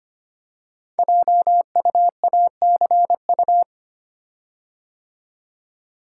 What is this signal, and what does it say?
Morse code, heard as a keyed tone, about 25 wpm, 700 Hz. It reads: JUACU
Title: Code Groups - 5 Letters - 25wpm